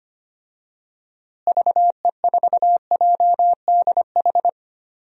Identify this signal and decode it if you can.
Morse code, heard as a keyed tone, about 25 wpm, 700 Hz. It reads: VE4JDH
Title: DXCC-100 Call Signs - 6 Characters - Repeated 2x - 25wpm